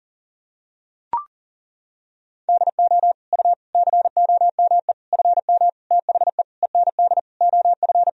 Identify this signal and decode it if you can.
Morse code, heard as a keyed tone, about 40 wpm, 700 Hz. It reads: DO U COME FM THE END OF
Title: Sentences from Top 200 Words and Abbreviations 40wpm